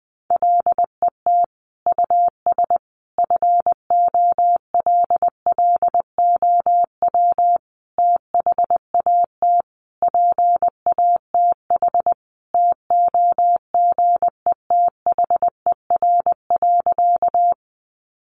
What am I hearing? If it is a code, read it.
Morse code, heard as a keyed tone, about 20 wpm, 700 Hz. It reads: LET US FOLLOW THAT PATH TOGETHER.
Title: Sentences from Top 1400 Words 20wpm